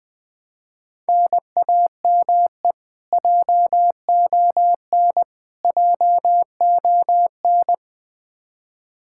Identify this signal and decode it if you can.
Morse code, heard as a keyed tone, about 20 wpm, 700 Hz. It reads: NAME JON JON
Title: K1USN SST QSO Practice 20wpm